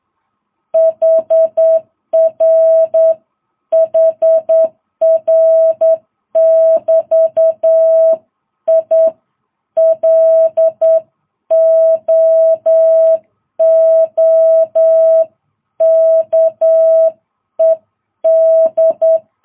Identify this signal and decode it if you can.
Morse code, heard as a keyed tone, about 9 wpm, 650 Hz. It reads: HR HR <BT> I LOOKED
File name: jh8cbh